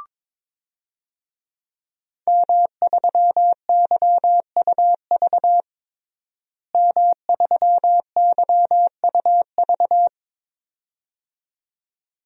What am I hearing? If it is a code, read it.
Morse code, heard as a keyed tone, about 21 wpm, 700 Hz. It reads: M3YUV M3YUV
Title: DXCC-100 Call Signs - 5 Characters - Repeated 2x - 22wpm